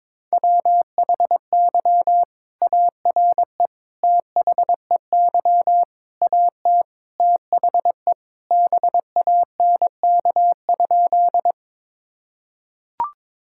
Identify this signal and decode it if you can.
Morse code, heard as a keyed tone, about 22 wpm, 700 Hz. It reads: WHY ARE THEY AT THE BANK?
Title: Sentences from Top 1000 Words - Review Part 6 - 22wpm